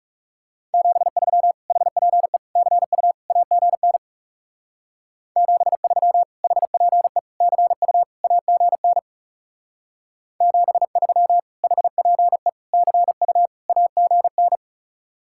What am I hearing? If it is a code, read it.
Morse code, heard as a keyed tone, about 34 wpm, 700 Hz. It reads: 73 HPE CU AGN 73 HPE CU AGN 73 HPE CU AGN
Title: Speed-Racing Common QSO Phrases - 30wpm